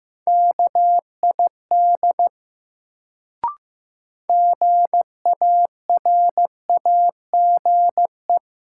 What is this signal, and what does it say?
Morse code, heard as a keyed tone, about 15 wpm, 700 Hz. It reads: KID GARAGE
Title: Sentences from Top 1600 Words - New Word Drill - 15wpm